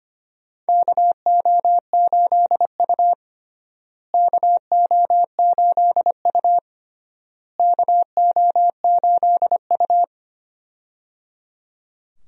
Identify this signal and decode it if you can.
Morse code, heard as a keyed tone, about 25 wpm, 700 Hz. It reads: KO8U KO8U KO8U
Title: USA Call Signs - 4 Characters - Repeated 3x - 25wpm